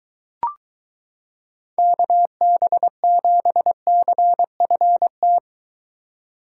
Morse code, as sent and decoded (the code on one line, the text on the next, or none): -.- -... --... -.-. ..-. -
KB7CFT